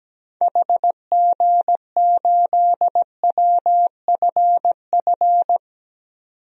Morse code, heard as a keyed tone, about 17 wpm, 700 Hz. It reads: HG8WFF
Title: DXCC-100 Call Signs - 6 Characters - 17wpm